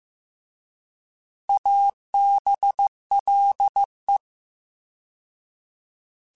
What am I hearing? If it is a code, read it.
Morse code, heard as a keyed tone, about 15 wpm, 800 Hz. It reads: ABLE